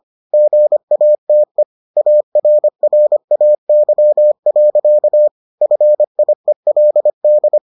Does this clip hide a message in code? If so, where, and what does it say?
Yes: Morse code from 0.3 to 7.6 s, heard as a keyed tone, about 25 wpm, 600 Hz. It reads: GATE ARRAY. FIELD